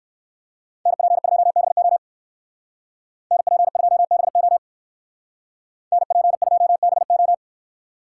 Morse code, heard as a keyed tone, about 64 wpm, 700 Hz. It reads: NP2BO NP2BO NP2BO